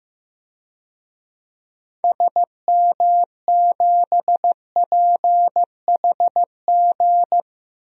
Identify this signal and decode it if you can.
Morse code, heard as a keyed tone, about 15 wpm, 700 Hz. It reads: SM7PHG
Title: DXCC-100 Call Signs - 6 Characters - 15wpm